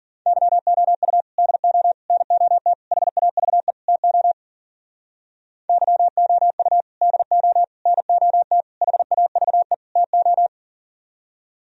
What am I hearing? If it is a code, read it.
Morse code, heard as a keyed tone, about 51 wpm, 700 Hz. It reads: YOU DO NOT HAVE TO YOU DO NOT HAVE TO